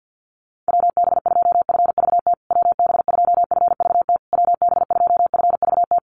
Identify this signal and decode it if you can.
Morse code, heard as a keyed tone, about 50 wpm, 700 Hz. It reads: WB2FVT WB2FVT WB2FVT
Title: USA Call Signs Repeated 3x 50wpm